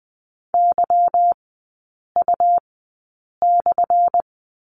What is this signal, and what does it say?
Morse code, heard as a keyed tone, about 20 wpm, spaced as for 8 wpm, 700 Hz. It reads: YU/
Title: 178 Lesson 29e - Slash - Three Characters 20wpm